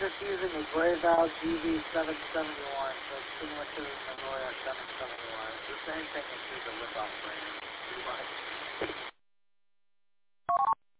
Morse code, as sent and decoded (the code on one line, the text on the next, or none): none